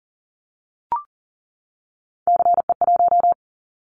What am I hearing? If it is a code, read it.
Morse code, heard as a keyed tone, about 40 wpm, 700 Hz. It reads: CE1
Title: Callsign Prefixes for Top 100 DXCC - 40wpm